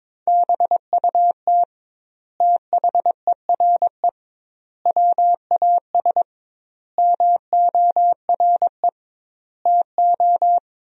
Morse code, heard as a keyed tone, about 21 wpm, 700 Hz. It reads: BUT THERE WAS MORE TO
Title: 2x Word-Spacing - Sentences from Top 100 Words - 22wpm